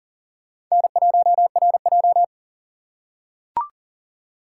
Morse code, heard as a keyed tone, about 38 wpm, 700 Hz. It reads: N1RJ
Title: USA Call Signs - 4 Characters - 2x Character Spacing - 40wpm